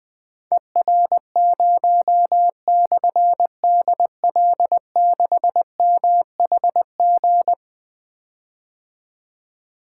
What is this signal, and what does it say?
Morse code, heard as a keyed tone, about 20 wpm, 700 Hz. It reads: ER0/DL6MHG
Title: Difficult Call Signs 20wpm